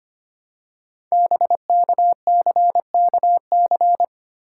Morse code, heard as a keyed tone, about 25 wpm, 690 Hz. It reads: BKCKC
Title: Code Groups - 5 Letters - 25wpm